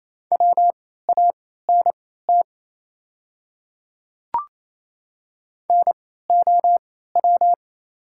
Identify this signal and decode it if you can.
Morse code, heard as a keyed tone, about 28 wpm, spaced as for 16 wpm, 690 Hz. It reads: WANT NOW
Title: 3x Character Spacing - Top 300 Words - 28wpm